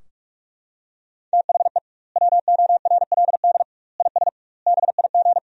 Morse code, heard as a keyed tone, about 42 wpm, 700 Hz. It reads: THE WORLD IS BIG
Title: 2x Word-Spacing - Sentences from Top 400 Words - 45wpm